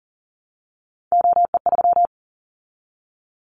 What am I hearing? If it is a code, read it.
Morse code, heard as a keyed tone, about 40 wpm, 700 Hz. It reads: OE3
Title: Callsign Prefixes for Top 25 DXCC - 40wpm